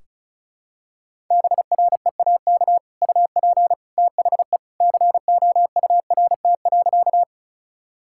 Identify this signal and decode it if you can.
Morse code, heard as a keyed tone, about 34 wpm, 700 Hz. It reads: BREAK UP THE COURT.
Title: Sentences from Top 1600 Words 35wpm